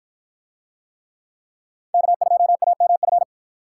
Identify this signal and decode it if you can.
Morse code, heard as a keyed tone, about 48 wpm, 700 Hz. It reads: K2AMF